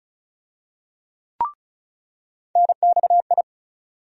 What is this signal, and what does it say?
Morse code, heard as a keyed tone, about 35 wpm, 700 Hz. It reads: NXI
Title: USA Call Sign Postfixes - 3 Letters 35wpm